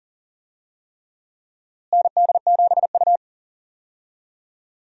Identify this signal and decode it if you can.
Morse code, heard as a keyed tone, about 40 wpm, 700 Hz. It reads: ND7U
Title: USA Call Signs - 4 Characters - Repeated 3x - 40wpm